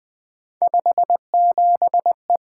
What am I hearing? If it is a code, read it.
Morse code, heard as a keyed tone, about 20 wpm, 700 Hz. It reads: 57E
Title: Three Letter-Number Combinations - 20wpm